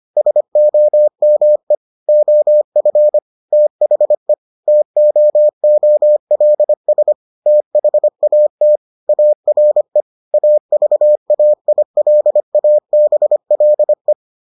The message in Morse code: SOME OF THE TOOLS THAT ARE AVAILABLE

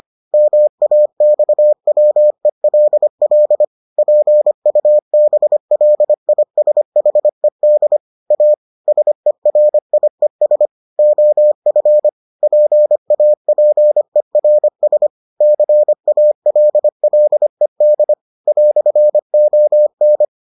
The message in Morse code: MAXWELL PUBLISHED A SERIES OF PAPERS CALLED "ON

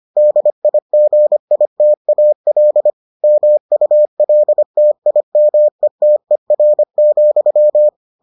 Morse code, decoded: DIGITAL MULTIMETER,

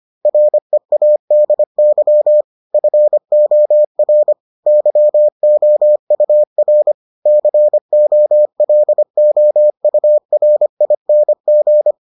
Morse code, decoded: READY FOR YOUR COLOURING

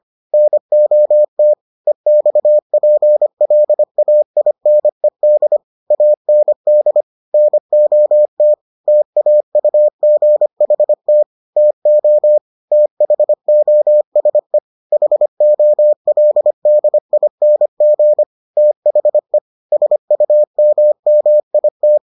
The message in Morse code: NOT EXPLAINED AND NOT TAUGHT TO THOSE HOLDING THE SUMMIT

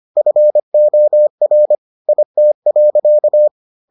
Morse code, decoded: FOR IT.

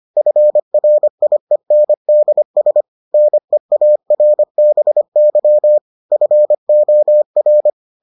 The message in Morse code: FRIENDS NEARBY FOR

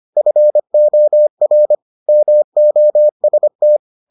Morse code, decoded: FOR MOST

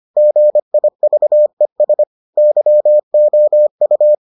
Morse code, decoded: GIVES YOU

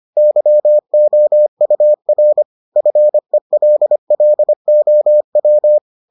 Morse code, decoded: YOUR FELLOW

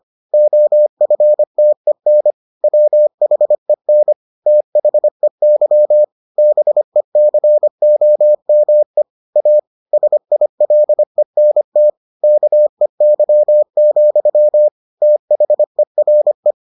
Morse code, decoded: OFTEN WHEN THEY BECOME A SILENT KEY, THERE

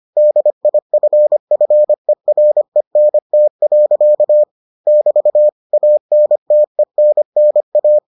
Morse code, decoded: DIFFERENT. <BT> ANTENNA

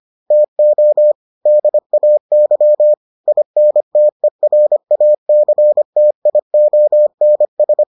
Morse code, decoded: TO DAY INTERACTIONS

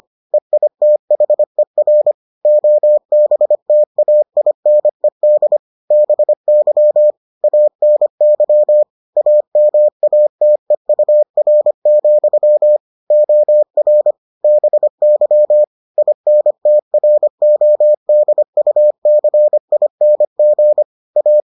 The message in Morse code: EITHER OBTAINED BY ANY AMATEUR, OR BY INTRODUCING A

